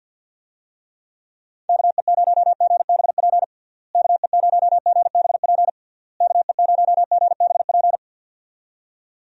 Morse code, heard as a keyed tone, about 51 wpm, 700 Hz. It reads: KE0GBP KE0GBP KE0GBP
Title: USA Call Signs - 6 Characters - Repeated 3x - 50wpm